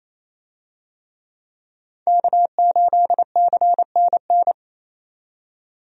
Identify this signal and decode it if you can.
Morse code, heard as a keyed tone, about 28 wpm, 700 Hz. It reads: K8CNN
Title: USA Call Signs - 5 Characters - 28wpm